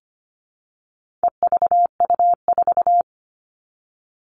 Morse code, heard as a keyed tone, about 25 wpm, 700 Hz. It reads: EVU4
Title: All Cut Numbers - 4-Digit 25wpm